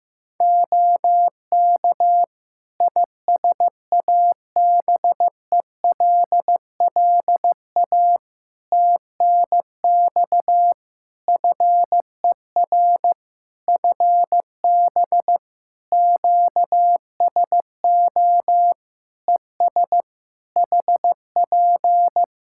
Morse code, heard as a keyed tone, about 16 wpm, 700 Hz. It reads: OK ISABELLA TNX FER FB QSO ES HP
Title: Practice QSO - Exchange 4 - Ending - 15wpm